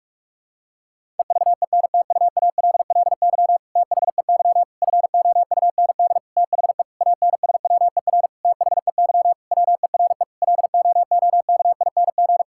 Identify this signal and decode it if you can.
Morse code, heard as a keyed tone, about 47 wpm, 700 Hz. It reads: EVENTUALLY THEY FOUND THE ANSWER THEY WERE LOOKING
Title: Sentences from Top 1000 Words - Review Part 6 - 45wpm